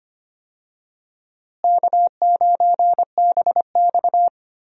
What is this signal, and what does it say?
Morse code, heard as a keyed tone, about 25 wpm, 700 Hz. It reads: K9BX
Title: USA Call Signs - 4 Characters - Repeated 2x - 25wpm